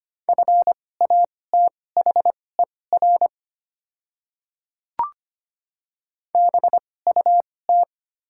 Morse code, heard as a keyed tone, about 25 wpm, spaced as for 18 wpm, 700 Hz. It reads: FATHER BUT
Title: 2x Character Spacing - Top 200 Words - 25wpm